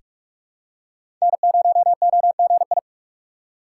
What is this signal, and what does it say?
Morse code, heard as a keyed tone, about 46 wpm, 700 Hz. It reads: N0OGI
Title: USA Call Signs - 5 Characters - 6x Character Spacing - 45wpm